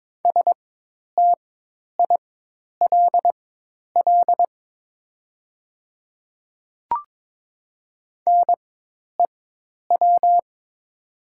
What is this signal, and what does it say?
Morse code, heard as a keyed tone, about 22 wpm, spaced as for 10 wpm, 700 Hz. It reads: STILL NEW
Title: 4x Character Spacing - Top 500 Words - 22wpm